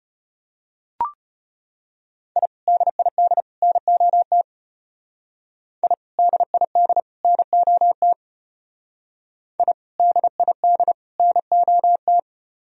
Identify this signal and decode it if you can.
Morse code, heard as a keyed tone, about 33 wpm, 700 Hz. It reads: I DID NOT I DID NOT I DID NOT